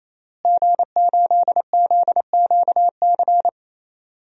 Morse code, heard as a keyed tone, about 28 wpm, 700 Hz. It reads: G8ZQC
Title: DXCC-100 Call Signs - 5 Characters - Repeated 3x - 28wpm